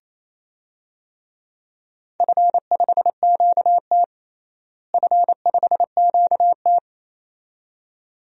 Morse code, heard as a keyed tone, about 27 wpm, 700 Hz. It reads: F5QT F5QT
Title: DXCC-100 Call Signs - 4 Characters - Repeated 2x - 28wpm